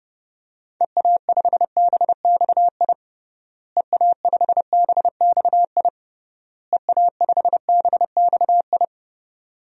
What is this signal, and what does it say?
Morse code, heard as a keyed tone, about 30 wpm, 700 Hz. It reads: EA5BXI EA5BXI EA5BXI
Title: DXCC-100 Call Signs - 6 Characters - Repeated 3x - 30wpm